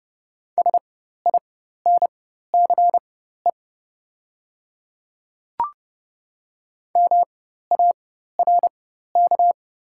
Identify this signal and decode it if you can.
Morse code, heard as a keyed tone, about 30 wpm, spaced as for 14 wpm, 700 Hz. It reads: SINCE MARK
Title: 4x Character Spacing - Top 300 Words - 30wpm